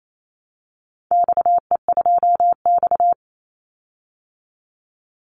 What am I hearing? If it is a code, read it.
Morse code, heard as a keyed tone, about 28 wpm, 700 Hz. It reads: XE2X
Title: Call Signs Repeated 3x - 28wpm